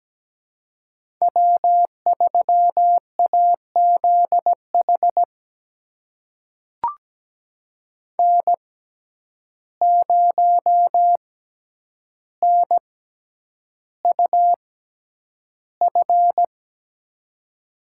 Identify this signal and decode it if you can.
Morse code, heard as a keyed tone, about 17 wpm, spaced as for 9 wpm, 700 Hz. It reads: W3AZH N0NUF
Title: USA Call Signs - 5 Characters - 6x Character Spacing - 17wpm